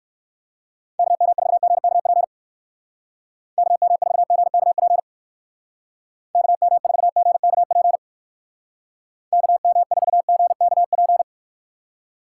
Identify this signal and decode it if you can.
Morse code, heard as a keyed tone, about 54 wpm, 700 Hz. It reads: KM4GKP KM4GKP KM4GKP KM4GKP
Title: Speed Racing - USA Call Signs 45wpm